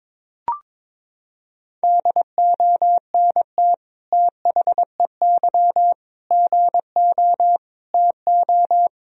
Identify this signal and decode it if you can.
Morse code, heard as a keyed tone, about 22 wpm, 700 Hz. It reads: DONT THEY GO TO
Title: Sentences from Top 1000 Words - Review Part 1 - 22wpm